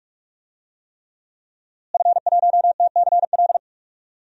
Morse code, heard as a keyed tone, about 44 wpm, 700 Hz. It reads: F1TCL
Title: DXCC-100 Call Signs - 5 Characters - Repeated 3x - 45wpm